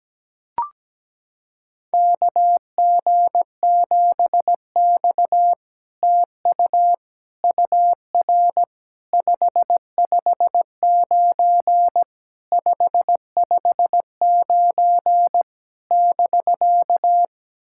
Morse code, heard as a keyed tone, about 17 wpm, 700 Hz. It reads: KG7X TU UR 559 559 <BK>
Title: SOTA QSO Practice 17wpm